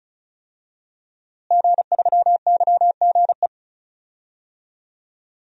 Morse code, heard as a keyed tone, about 34 wpm, 700 Hz. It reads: G3YGE
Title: DXCC-100 Call Signs - 5 Characters - Repeated 2x - 35wpm